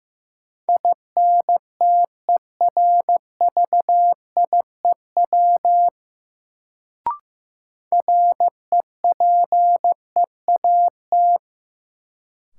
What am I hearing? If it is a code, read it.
Morse code, heard as a keyed tone, about 15 wpm, 700 Hz. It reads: INTERVIEW REPEAT
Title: Sentences from Top 1600 Words - New Word Drill - 15wpm